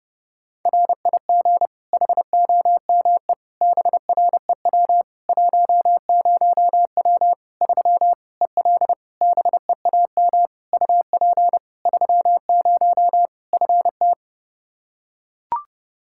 Morse code, heard as a keyed tone, about 30 wpm, 700 Hz. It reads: RIG HOME BREW 10W 3 EL BEAM UP 30 FT
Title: Practice QSO - Exchange 2 - Info - 30wpm